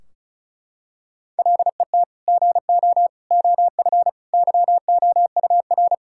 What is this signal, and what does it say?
Morse code, heard as a keyed tone, about 34 wpm, 700 Hz. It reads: LET GO OF YOUR
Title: Sentences from Top 1600 Words 35wpm